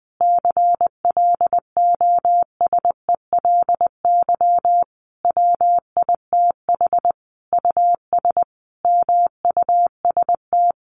Morse code, heard as a keyed tone, about 20 wpm, 700 Hz. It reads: CLOSELY WITH US MUST
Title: Sentences from Top 1400 Words 20wpm